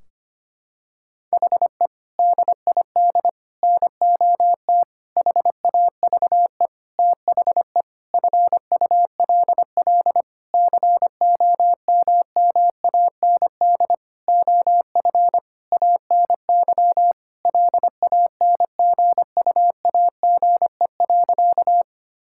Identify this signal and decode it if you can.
Morse code, heard as a keyed tone, about 25 wpm, 700 Hz. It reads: HE DID NOT HAVE THE FULL COMMAND OF ANY LANGUAGE.
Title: Sentences from Top 1600 Words 25wpm